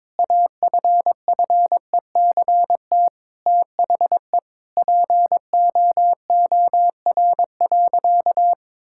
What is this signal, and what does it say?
Morse code, heard as a keyed tone, about 22 wpm, 700 Hz. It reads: AFFECT THE POOR.
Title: Sentences from Top 1000 Words - Review Part 6 - 22wpm